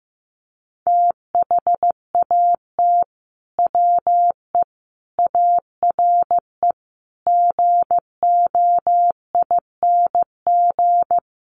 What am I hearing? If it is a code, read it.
Morse code, heard as a keyed tone, about 15 wpm, 700 Hz. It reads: THAT WE ARE GOING
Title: Sets of 5 Words Encore 15wpm